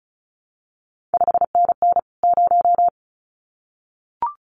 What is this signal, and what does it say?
Morse code, heard as a keyed tone, about 36 wpm, 700 Hz. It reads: 5NN 0
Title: CQ WW DX Contest Exchange Encore - 35wpm